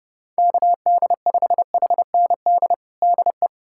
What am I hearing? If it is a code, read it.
Morse code, heard as a keyed tone, about 32 wpm, 700 Hz. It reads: KD5HND DE
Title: USA Callsign DE Callsign - 6 Characters - 30wpm